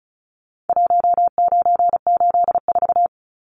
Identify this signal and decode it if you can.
Morse code, heard as a keyed tone, about 34 wpm, 700 Hz. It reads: 1984
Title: Numbers - Contest Style - 4-Digit 35wpm